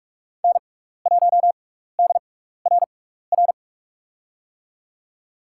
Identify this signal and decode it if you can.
Morse code, heard as a keyed tone, about 47 wpm, spaced as for 16 wpm, 700 Hz. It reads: N1DRR
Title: USA Call Signs - 6x Character Spacing - 45wpm